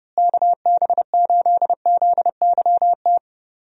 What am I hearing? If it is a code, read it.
Morse code, heard as a keyed tone, about 30 wpm, 700 Hz. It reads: KB8ZYT
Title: USA Call Signs - 6 Characters - Repeated 3x - 30wpm